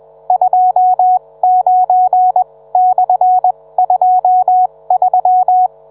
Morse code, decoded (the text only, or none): 29/23